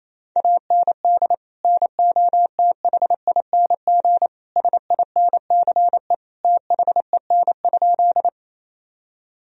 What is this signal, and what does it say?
Morse code, heard as a keyed tone, about 28 wpm, 700 Hz. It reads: AND NOTHING SINCE THEN?
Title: Sentences from Top 1000 Words - Review Part 1 - 28wpm